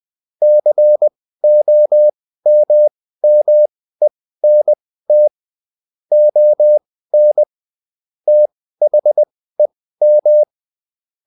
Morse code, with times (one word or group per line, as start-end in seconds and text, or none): 0.4-5.3 COMMENT
6.1-7.4 ON
8.3-10.4 THEM